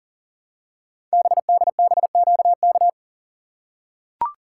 Morse code, heard as a keyed tone, about 40 wpm, 700 Hz. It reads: BDBQK
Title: Code Groups - 5 Letters - 40wpm